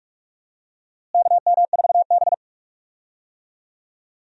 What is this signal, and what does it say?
Morse code, heard as a keyed tone, about 47 wpm, 700 Hz. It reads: KM4B
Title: USA Call Signs - 4 Characters - Repeated 3x - 45wpm